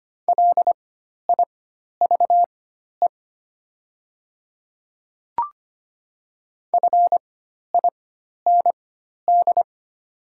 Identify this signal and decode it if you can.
Morse code, heard as a keyed tone, about 25 wpm, spaced as for 12 wpm, 700 Hz. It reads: LIVE FIND
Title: 4x Character Spacing - Top 200 Words - 25wpm